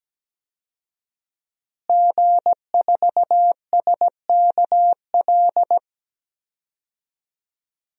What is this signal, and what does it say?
Morse code, heard as a keyed tone, about 17 wpm, 700 Hz. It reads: G4SKL
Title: DXCC-100 Call Signs - 5 Characters - Repeated 3x - 17wpm